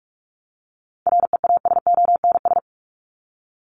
Morse code, heard as a keyed tone, about 44 wpm, 700 Hz. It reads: REASONS
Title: ES:S Suffix - Top 100 Words with Base Words 45wpm